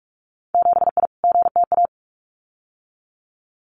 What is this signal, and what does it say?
Morse code, heard as a keyed tone, about 46 wpm, 700 Hz. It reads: 7I GTA
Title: Winter Field Day Exchanges Encore 45wpm